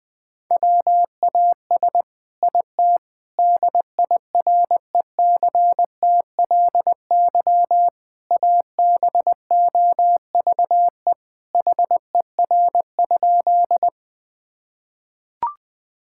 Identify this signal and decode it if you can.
Morse code, heard as a keyed tone, about 20 wpm, 700 Hz. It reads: WAS IT DIRECTLY ABOVE HER?
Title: Sentences from Top 1000 Words - Review Part 5 - 20wpm